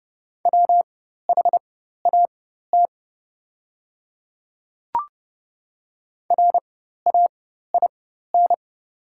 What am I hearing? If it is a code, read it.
Morse code, heard as a keyed tone, about 29 wpm, spaced as for 14 wpm, 700 Hz. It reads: WHAT RAIN
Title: 4x Character Spacing - Top 300 Words - 30wpm